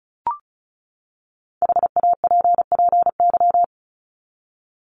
Morse code, heard as a keyed tone, about 34 wpm, 700 Hz. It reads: HAPPY
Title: Top 751-1000 Words 35wpm